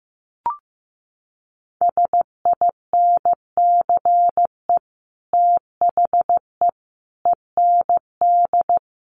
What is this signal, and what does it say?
Morse code, heard as a keyed tone, about 15 wpm, 700 Hz. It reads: SINCE THE END
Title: Sets of 5 Words Encore 15wpm